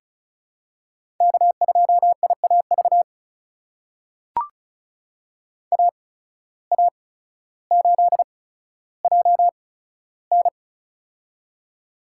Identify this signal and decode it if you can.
Morse code, heard as a keyed tone, about 36 wpm, spaced as for 15 wpm, 700 Hz. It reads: K2IAV AA8JN
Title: USA Call Signs - 5 Characters - 8x Character Spacing - 35wpm